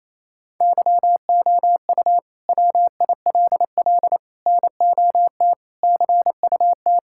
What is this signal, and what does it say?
Morse code, heard as a keyed tone, about 28 wpm, 700 Hz. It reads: YOU WILL NOT CUT
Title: Sentences from Top 1000 Words - Review Part 6 - 28wpm